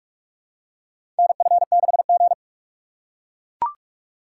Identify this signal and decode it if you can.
Morse code, heard as a keyed tone, about 42 wpm, 700 Hz. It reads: NF6G
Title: USA Call Signs - 3x Character Spacing - 45wpm